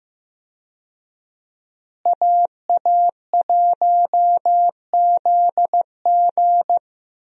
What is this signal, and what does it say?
Morse code, heard as a keyed tone, about 15 wpm, 700 Hz. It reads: AA1ZG
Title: USA Call Signs - 5 Characters - Repeated 3x - 15wpm